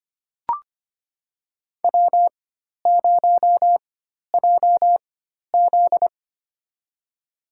Morse code, heard as a keyed tone, about 25 wpm, spaced as for 12 wpm, 690 Hz. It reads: W0JZ
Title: USA Call Signs - 4 Characters - 4x Character Spacing - 25wpm